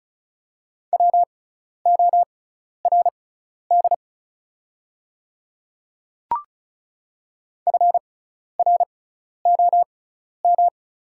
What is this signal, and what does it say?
Morse code, heard as a keyed tone, about 34 wpm, spaced as for 12 wpm, 700 Hz. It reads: WORD FROM
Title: 6x Character Spacing - Top 100 Words - 35wpm